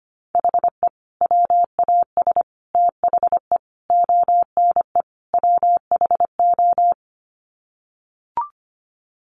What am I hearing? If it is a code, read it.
Morse code, heard as a keyed tone, about 25 wpm, 700 Hz. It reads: HE WAS THE ONE WHO
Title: Sets of 5 Words Encore2 25wpm